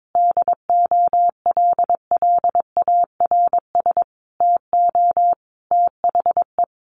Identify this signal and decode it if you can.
Morse code, heard as a keyed tone, about 22 wpm, 700 Hz. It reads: DOLLARS TO THE
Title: A Sound Of Thunder 22wpm